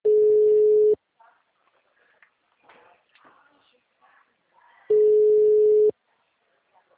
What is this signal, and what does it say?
Ringback tone (Europe, 425 Hz), with speech.